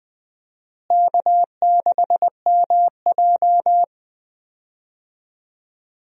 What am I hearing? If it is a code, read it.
Morse code, heard as a keyed tone, about 21 wpm, 700 Hz. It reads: K6MJ
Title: USA Call Signs - 4 Characters - Repeated 3x - 20wpm